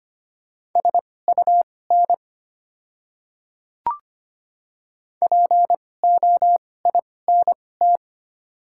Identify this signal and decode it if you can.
Morse code, heard as a keyed tone, about 25 wpm, spaced as for 18 wpm, 700 Hz. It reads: SUN POINT
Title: 2x Character Spacing - Top 300 Words - 25wpm